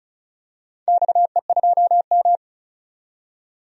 Morse code, heard as a keyed tone, about 34 wpm, 700 Hz. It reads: XE2M
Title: DXCC-100 Call Signs - 4 Characters - 35wpm